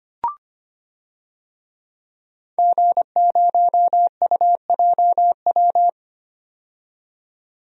Morse code, heard as a keyed tone, about 25 wpm, 700 Hz. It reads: G0UJW
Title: DXCC-100 Call Signs - 5 Characters - 25wpm